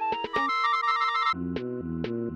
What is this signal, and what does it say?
There is no code.